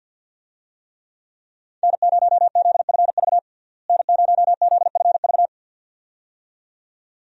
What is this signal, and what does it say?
Morse code, heard as a keyed tone, about 50 wpm, 700 Hz. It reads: N0ZFV N0ZFV
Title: USA Call Signs - 5 Characters - Repeated 2x - 50wpm